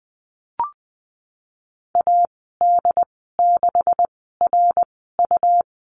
Morse code, heard as a keyed tone, about 20 wpm, spaced as for 15 wpm, 700 Hz. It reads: AD6RU
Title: 154 Lesson 25g - Letter p - Call Signs 20wpm